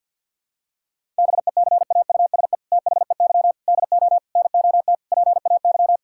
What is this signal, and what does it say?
Morse code, heard as a keyed tone, about 51 wpm, 690 Hz. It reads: BECAUSE THEY DO NOT PAY
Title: Sentences from Top 1000 Words - Review Part 5 - 50wpm